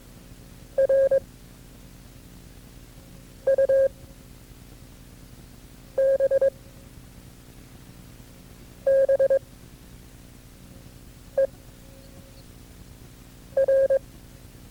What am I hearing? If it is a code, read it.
Morse code, heard as a keyed tone, about 22 wpm, 540 Hz. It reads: R U B B E R
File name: FreeFiles / CW-practice / KB6MT-tape2-side1